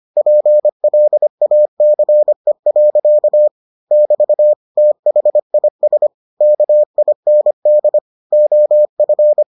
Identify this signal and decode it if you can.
Morse code, heard as a keyed tone, about 25 wpm, 600 Hz. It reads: PLACE. <BT> THIS KIND OF